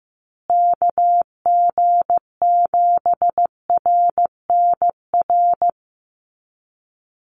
Morse code, heard as a keyed tone, about 15 wpm, 700 Hz. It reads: KG7RNR
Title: USA Call Signs Repeated 2x 15wpm